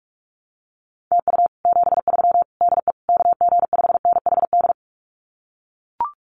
Morse code, heard as a keyed tone, about 44 wpm, 700 Hz. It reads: TU 73 DE KG5NHD
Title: Common QSO Phrases Encore 45wpm